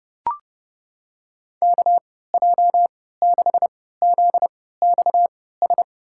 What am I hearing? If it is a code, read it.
Morse code, heard as a keyed tone, about 32 wpm, spaced as for 17 wpm, 700 Hz. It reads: KJ6ZXS
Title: USA Call Signs - 3x Character Spacing - 30wpm